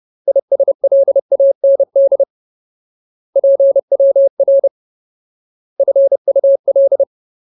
Morse code, heard as a keyed tone, about 30 wpm, 550 Hz. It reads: ISLAND PWR FUL
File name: Ultimate Word List / 30WPM Ultimate